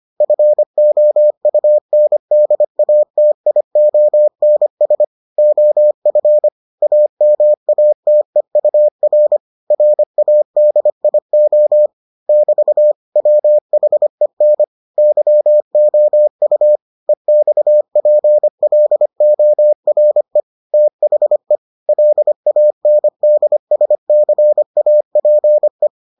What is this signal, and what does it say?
Morse code, heard as a keyed tone, about 25 wpm, 600 Hz. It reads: FOUNDATIONS OF AMATEUR RADIO <BT> WHEN YOU EXPLORE THE LANDSCAPE